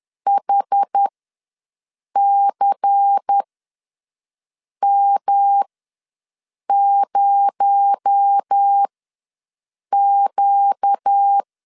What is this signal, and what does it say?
Morse code, heard as a keyed tone, about 10 wpm, spaced as for 5 wpm, 800 Hz. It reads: HCM0Q
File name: LEZIONI / 6 PKX0 / 6 PKX0 ws 10wpm 4220